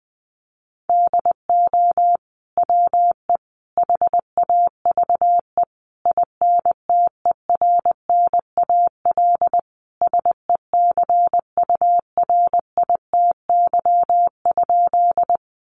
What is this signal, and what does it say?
Morse code, heard as a keyed tone, about 20 wpm, 700 Hz. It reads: DO WE HAVE INTERNAL SECURITY?
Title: Sentences from Top 1400 Words 20wpm